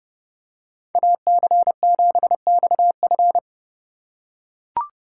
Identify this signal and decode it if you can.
Morse code, heard as a keyed tone, about 28 wpm, 700 Hz. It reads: AC7XF